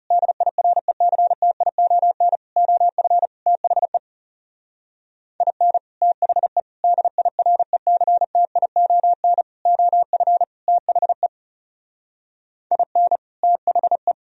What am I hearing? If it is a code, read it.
Morse code, heard as a keyed tone, about 35 wpm, 700 Hz. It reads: DIRECTION OF THE IN THE DIRECTION OF THE IN THE
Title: Speed-Racing - Sets of 5 Words 30wpm